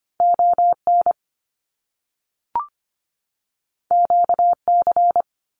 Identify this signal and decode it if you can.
Morse code, heard as a keyed tone, about 25 wpm, 700 Hz. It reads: ON QC
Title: Canadian Provinces and Territories 25wpm